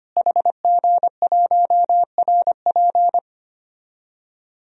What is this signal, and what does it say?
Morse code, heard as a keyed tone, about 25 wpm, 700 Hz. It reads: HG1RP
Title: DXCC-100 Call Signs - 5 Characters - Repeated 3x - 25wpm